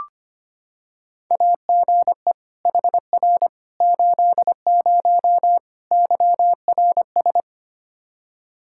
Morse code, heard as a keyed tone, about 25 wpm, 700 Hz. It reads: AGE HR 80 YRS
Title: K1USN SST QSO Practice 25wpm